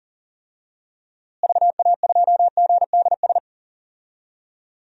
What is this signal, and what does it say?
Morse code, heard as a keyed tone, about 40 wpm, 700 Hz. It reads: VA2GDS
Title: DXCC-100 Call Signs - 6 Characters - Repeated 2x - 40wpm